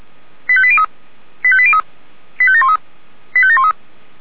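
Five-tone selective call (ZVEI), heard as quick runs of five tones, tones of about 70 ms. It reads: 85803 85803 85713 85713
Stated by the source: ZVEI five-tone selective call, code 85827, received by SDR.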